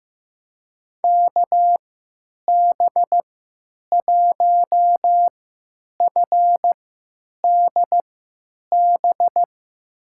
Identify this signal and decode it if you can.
Morse code, heard as a keyed tone, about 16 wpm, spaced as for 9 wpm, 700 Hz. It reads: KB1FDB
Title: USA Call Signs - 6 Characters - 3x Character Spacing - 15wpm